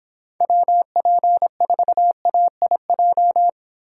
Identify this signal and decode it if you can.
Morse code, heard as a keyed tone, about 29 wpm, 700 Hz. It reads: WP4AIJ